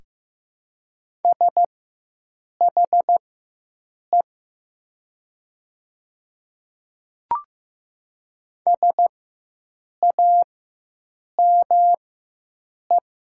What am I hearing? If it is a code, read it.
Morse code, heard as a keyed tone, about 15 wpm, spaced as for 7 wpm, 700 Hz. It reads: SHE SAME
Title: 4x Character Spacing - Top 300 Words - 15wpm